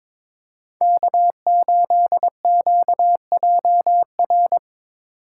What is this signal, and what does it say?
Morse code, heard as a keyed tone, about 21 wpm, 700 Hz. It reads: K8QJR
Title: USA Call Signs - 5 Characters - 4x Character Spacing - 22wpm